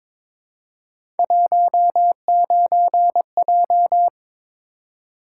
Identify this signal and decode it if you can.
Morse code, heard as a keyed tone, about 22 wpm, 700 Hz. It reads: 19J
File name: practice / 22wpm / Three Letter-Number Combinations - 22wpm